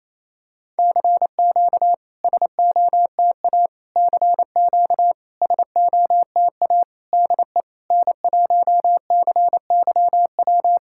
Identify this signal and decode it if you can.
Morse code, heard as a keyed tone, about 27 wpm, 690 Hz. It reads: CQ SOTA CQ SOTA DE N1CYW
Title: SOTA QSO Practice 28wpm